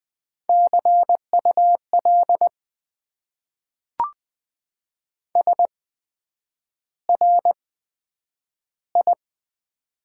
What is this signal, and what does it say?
Morse code, heard as a keyed tone, about 20 wpm, spaced as for 9 wpm, 700 Hz. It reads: CUL SRI
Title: Top 90 QSO Elements - 8x Character Spacing - 20wpm